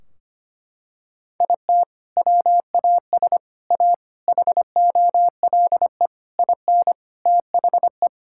Morse code, heard as a keyed tone, about 25 wpm, 700 Hz. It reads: IT WAS A HOLE IN THE
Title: Sentences from Top 1600 Words 25wpm